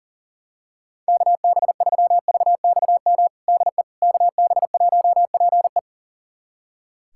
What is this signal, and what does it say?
Morse code, heard as a keyed tone, about 37 wpm, 700 Hz. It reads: KB3VXM DE KB1PE
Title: USA Callsign DE Callsign 40wpm